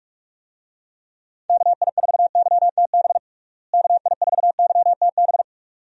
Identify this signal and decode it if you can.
Morse code, heard as a keyed tone, about 46 wpm, 700 Hz. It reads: KI4YTB KI4YTB
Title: USA Call Signs - 6 Characters - Repeated 2x - 45wpm